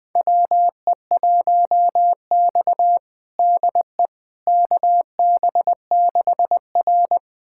Morse code, heard as a keyed tone, about 21 wpm, 700 Hz. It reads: WE1X DE KB6R
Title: USA Callsign DE Callsign - 4 Characters - 20wpm